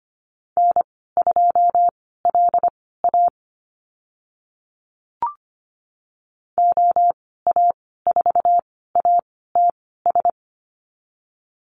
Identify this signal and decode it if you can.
Morse code, heard as a keyed tone, about 25 wpm, spaced as for 16 wpm, 700 Hz. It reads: N2LA OA4ATS
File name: Lesson 15g - Number 5 - Call Signs 25wpm